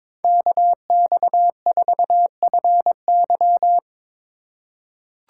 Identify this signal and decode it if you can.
Morse code, heard as a keyed tone, about 22 wpm, 700 Hz. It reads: KX4FY
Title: USA Call Signs - 5 Characters - 3x Character Spacing - 22wpm